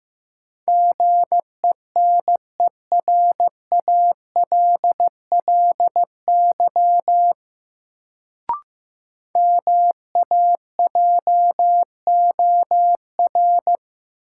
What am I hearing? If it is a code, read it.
Morse code, heard as a keyed tone, about 15 wpm, 700 Hz. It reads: GENERALLY MAJOR
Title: Sentences from Top 600 Words - New Word Drill - 15wpm